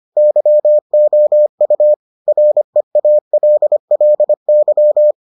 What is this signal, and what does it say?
Morse code, heard as a keyed tone, about 25 wpm, 600 Hz. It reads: YOU REALLY